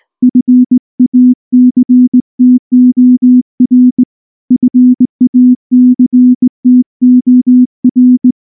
Morse code, heard as a keyed tone, about 19 wpm, 250 Hz. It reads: FACTOR FACTOR
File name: morse / morse-training-20WPM-6